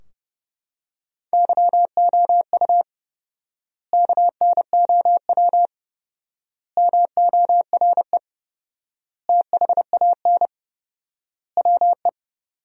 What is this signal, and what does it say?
Morse code, heard as a keyed tone, about 30 wpm, 700 Hz. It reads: YOU KNOW MORE THAN WE
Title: 4x Word-Spacing - Sentences from Top 100 Words - 30wpm